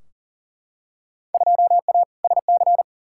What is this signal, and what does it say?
Morse code, heard as a keyed tone, about 39 wpm, 700 Hz. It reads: 2A SC
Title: Field Day Exchanges - 40wpm